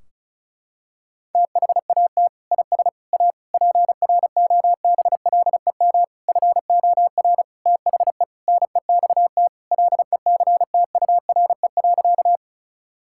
Morse code, heard as a keyed tone, about 34 wpm, 700 Hz. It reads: THAT IS A PROBLEM FOR THE NEXT LECTURE.
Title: Sentences from Top 1600 Words 35wpm